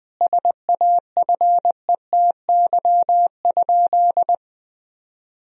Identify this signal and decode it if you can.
Morse code, heard as a keyed tone, about 21 wpm, 700 Hz. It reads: SAFETY?
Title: Sentences from Top 1000 Words - Review Part 4 - 20wpm